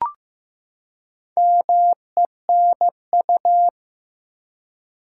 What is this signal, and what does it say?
Morse code, heard as a keyed tone, about 15 wpm, 700 Hz. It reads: MENU
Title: Four Letter Words - 15wpm